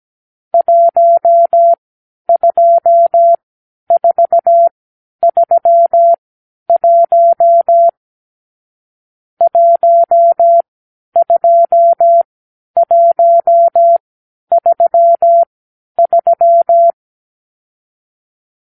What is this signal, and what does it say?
Morse code, heard as a keyed tone, about 17 wpm, spaced as for 10 wpm, 680 Hz. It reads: 12431 12133
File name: k7qo / Lesson 46 (1-4)